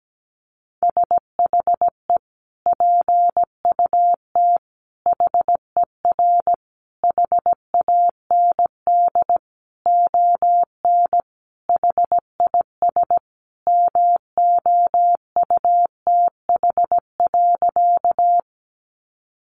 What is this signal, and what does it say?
Morse code, heard as a keyed tone, about 17 wpm, 700 Hz. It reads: SHE PUT HER HAND ON HIS MOUTH.
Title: Sentences from Top 1500 Words 17wpm